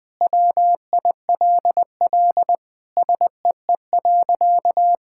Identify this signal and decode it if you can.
Morse code, heard as a keyed tone, about 19 wpm, 700 Hz. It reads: WILL SEE.
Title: Sentences with 3-Word Phrases - Spoken Beginning - 20wpm